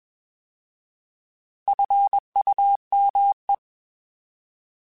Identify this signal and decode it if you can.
Morse code, heard as a keyed tone, about 21 wpm, 800 Hz. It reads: FUME
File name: Alphabetical / Alphabetical mp3s / Fs @20